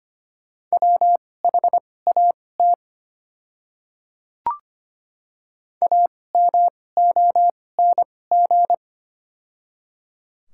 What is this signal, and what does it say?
Morse code, heard as a keyed tone, about 25 wpm, spaced as for 18 wpm, 690 Hz. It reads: WHAT AMONG
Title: 2x Character Spacing - Top 500 Words - 25wpm